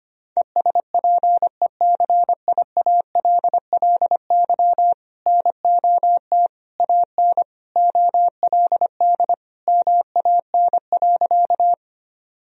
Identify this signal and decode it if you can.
Morse code, heard as a keyed tone, about 25 wpm, 700 Hz. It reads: ESPECIALLY NOT AN OLD MAN.
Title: Sentences from Top 1000 Words - Review Part 4 - 25wpm